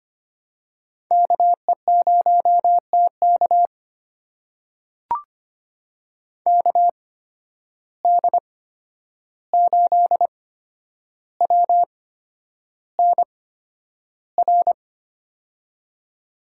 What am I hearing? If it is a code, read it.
Morse code, heard as a keyed tone, about 25 wpm, spaced as for 10 wpm, 700 Hz. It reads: KE0TK KD8WNR
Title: USA Call Signs - 8x Character Spacing - 25wpm